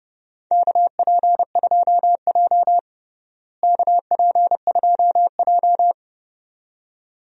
Morse code, heard as a keyed tone, about 32 wpm, 700 Hz. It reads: KP2J KP2J
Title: DXCC-100 Call Signs - 4 Characters - Repeated 2x - 30wpm